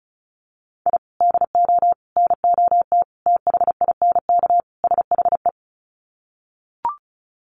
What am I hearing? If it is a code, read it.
Morse code, heard as a keyed tone, about 34 wpm, 700 Hz. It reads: I DO NOT THINK SHE
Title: Sets of 5 Words 35wpm